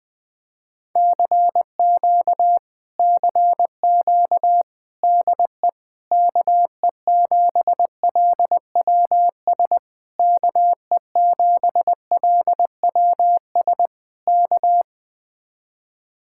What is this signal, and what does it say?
Morse code, heard as a keyed tone, about 20 wpm, 700 Hz. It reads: CQ CQ DE KE7LWS KE7LWS K
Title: Speed-Racing Common QSO Phrases - 17wpm